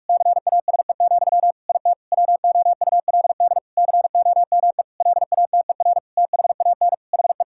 Morse code, heard as a keyed tone, about 46 wpm, 700 Hz. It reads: CASE, IT WOULD COME LATER THAN HE
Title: Sentences from Top 1600 Words 45wpm